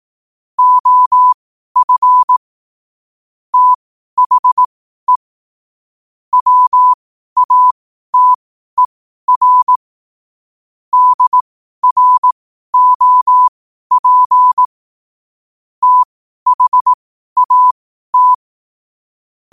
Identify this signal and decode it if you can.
Morse code, heard as a keyed tone, about 18 wpm, spaced as for 12 wpm, 1000 Hz. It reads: OF THE WATER DROP THAT
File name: qotdfiles / 12wpm